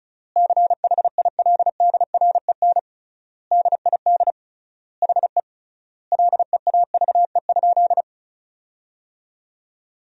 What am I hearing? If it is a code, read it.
Morse code, heard as a keyed tone, about 34 wpm, 700 Hz. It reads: CHILDREN DID HE LEAVE?
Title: 3x Word-Spacing - Sentences from Top 400 Words - 35wpm